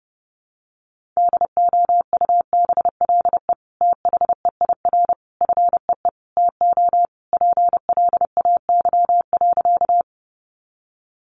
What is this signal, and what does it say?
Morse code, heard as a keyed tone, about 30 wpm, 700 Hz. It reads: DOUBLE THEIR FEE TO PLAY.
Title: Sentences from Top 1500 Words 30wpm